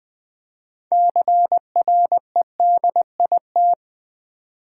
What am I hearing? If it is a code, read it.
Morse code, heard as a keyed tone, about 20 wpm, 700 Hz. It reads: CREDIT
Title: 6 Letter Words - First 3 Letters Given - 20wpm